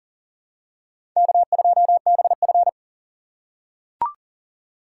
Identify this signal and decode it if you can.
Morse code, heard as a keyed tone, about 40 wpm, 700 Hz. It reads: K2BF
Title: USA Call Signs - 4 Characters - 2x Character Spacing - 40wpm